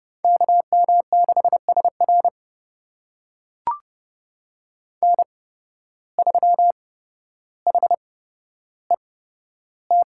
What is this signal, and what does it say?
Morse code, heard as a keyed tone, about 30 wpm, spaced as for 13 wpm, 700 Hz. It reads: KM6SR N3HET
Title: USA Call Signs - 5 Characters - 8x Character Spacing - 30wpm